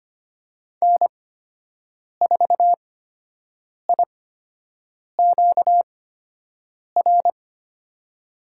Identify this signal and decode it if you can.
Morse code, heard as a keyed tone, about 25 wpm, spaced as for 7 wpm, 700 Hz. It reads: N4IQR
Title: USA Call Signs - 8x Character Spacing - 25wpm